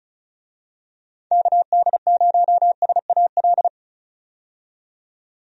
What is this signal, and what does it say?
Morse code, heard as a keyed tone, about 36 wpm, 700 Hz. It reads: KD0SAL
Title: USA Call Signs - 6 Characters - 35wpm